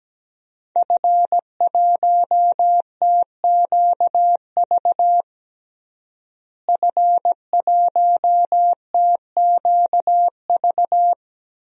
Morse code, heard as a keyed tone, about 17 wpm, 700 Hz. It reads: F1TQV F1TQV
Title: DXCC-100 Call Signs - 5 Characters - Repeated 2x - 17wpm